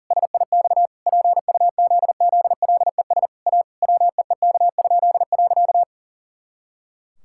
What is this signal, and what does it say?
Morse code, heard as a keyed tone, about 37 wpm, 700 Hz. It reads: SIX PUZZLES A WEEK?.
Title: Pangram Challenge v2 - 40wpm